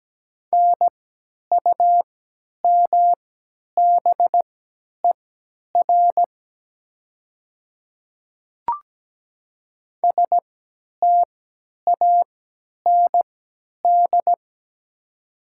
Morse code, heard as a keyed tone, about 17 wpm, spaced as for 10 wpm, 700 Hz. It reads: NUMBER STAND
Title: 3x Character Spacing - Top 300 Words - 17wpm